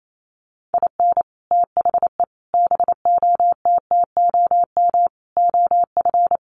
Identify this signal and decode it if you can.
Morse code, heard as a keyed tone, about 28 wpm, 700 Hz. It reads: IN THE BOTTOM OF
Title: Sets of 5 Words Encore 28wpm